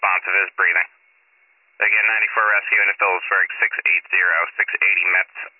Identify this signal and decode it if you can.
Emergency Alert System (SAME) tones (header bursts), heard as harsh buzzing bursts, with speech. It is heard from 0.4 to 4.7 s.